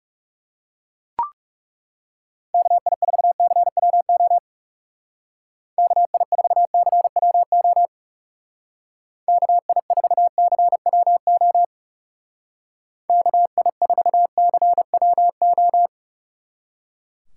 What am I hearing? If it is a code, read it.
Morse code, heard as a keyed tone, about 37 wpm, 700 Hz. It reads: KI4CWO KI4CWO KI4CWO KI4CWO
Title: Speed Racing - USA 2x3 Call Signs 30wpm